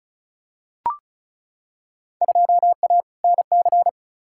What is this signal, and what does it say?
Morse code, heard as a keyed tone, about 34 wpm, 700 Hz. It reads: 2A NC
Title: Field Day Exchanges - 35wpm